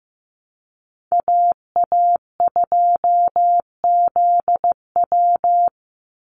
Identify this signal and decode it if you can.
Morse code, heard as a keyed tone, about 15 wpm, 700 Hz. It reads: AA2ZW
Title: Call Signs Repeated 2x - 15wpm